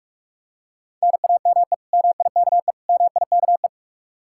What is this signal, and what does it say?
Morse code, heard as a keyed tone, about 46 wpm, 700 Hz. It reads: NAME MIKE MIKE